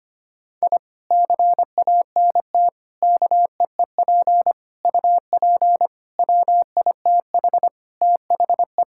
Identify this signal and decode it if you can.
Morse code, heard as a keyed tone, about 25 wpm, 700 Hz. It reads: I CANT KEEP UP WITH THE
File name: practice / 25wpm / Sentences with 2-Word Phrases - 25wpm